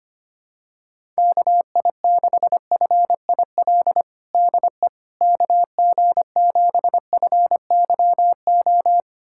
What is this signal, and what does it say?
Morse code, heard as a keyed tone, about 25 wpm, 700 Hz. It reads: KI6FIL DE KG7FYO
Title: USA Callsign DE Callsign 25wpm